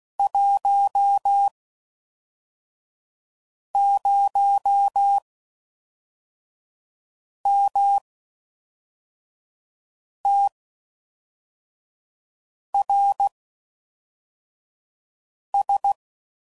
Morse code, heard as a keyed tone, about 16 wpm, spaced as for 4 wpm, 800 Hz. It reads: 10MTRS